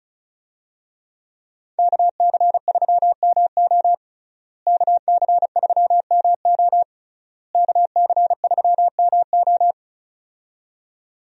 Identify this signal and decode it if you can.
Morse code, heard as a keyed tone, about 36 wpm, 700 Hz. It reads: KC3MO KC3MO KC3MO
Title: USA Call Signs - 5 Characters - Repeated 3x - 35wpm